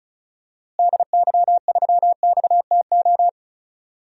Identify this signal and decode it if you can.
Morse code, heard as a keyed tone, about 36 wpm, 700 Hz. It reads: DY3XTO
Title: DXCC-100 Call Signs - 6 Characters - Repeated 3x - 35wpm